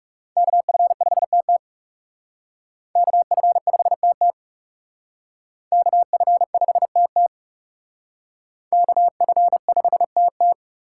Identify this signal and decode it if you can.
Morse code, heard as a keyed tone, about 37 wpm, 700 Hz. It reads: KF5TT KF5TT KF5TT KF5TT
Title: Speed Racing - USA 2x2 Call Signs 30wpm